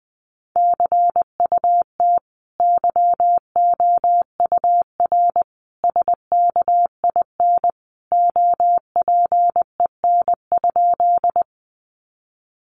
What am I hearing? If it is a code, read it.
Morse code, heard as a keyed tone, about 20 wpm, 700 Hz. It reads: CUT YOUR SKIN OPEN?
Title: Sentences from Top 1300 Words 20wpm